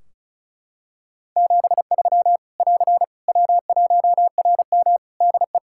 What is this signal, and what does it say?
Morse code, heard as a keyed tone, about 34 wpm, 700 Hz. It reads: 73 <AR> W1RM DE
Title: Practice QSO - Exchange 4 - Ending - 35wpm